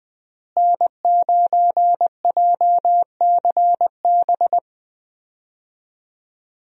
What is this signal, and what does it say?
Morse code, heard as a keyed tone, about 20 wpm, 700 Hz. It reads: N9JCB
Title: USA Call Signs - 5 Characters - Repeated 3x - 20wpm